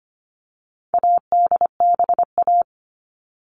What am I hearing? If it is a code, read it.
Morse code, heard as a keyed tone, about 25 wpm, 700 Hz. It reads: ADBA
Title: All Cut Numbers - 4-Digit 25wpm